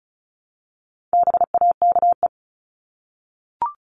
Morse code, heard as a keyed tone, about 34 wpm, 700 Hz. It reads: BAKE
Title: Top 1751-2000 Words 35wpm